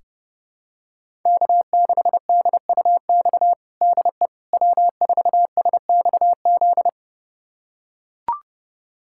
Morse code, heard as a keyed tone, about 30 wpm, 700 Hz. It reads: K6DUX DE W4SXZ
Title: USA Callsign DE Callsign - 5 Characters - 30wpm